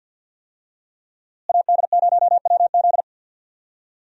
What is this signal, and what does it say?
Morse code, heard as a keyed tone, about 46 wpm, 700 Hz. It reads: AD0WZ
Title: USA Call Signs - 5 Characters - Repeated 2x - 50wpm